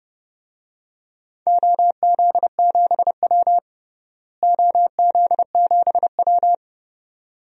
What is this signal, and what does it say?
Morse code, heard as a keyed tone, about 30 wpm, 700 Hz. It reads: OZ7W OZ7W
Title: DXCC-100 Call Signs - 4 Characters - Repeated 2x - 30wpm